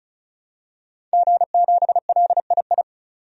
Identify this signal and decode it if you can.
Morse code, heard as a keyed tone, about 34 wpm, 700 Hz. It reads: G7LII
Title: DXCC-100 Call Signs - 5 Characters - 35wpm